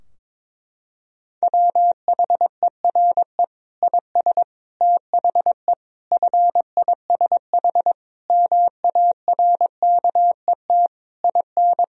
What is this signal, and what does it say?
Morse code, heard as a keyed tone, about 22 wpm, 700 Hz. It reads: WHERE IS THE FISH MARKET IN
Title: Sentences from Top 1000 Words - Review Part 3 - 22wpm